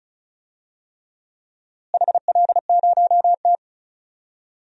Morse code, heard as a keyed tone, about 34 wpm, 700 Hz. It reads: HL0T
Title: DXCC-100 Call Signs - 4 Characters - Repeated 3x - 35wpm